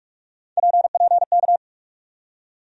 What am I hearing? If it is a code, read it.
Morse code, heard as a keyed tone, about 45 wpm, 700 Hz. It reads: PPK
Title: USA Call Sign Postfixes - 3 Letters 45wpm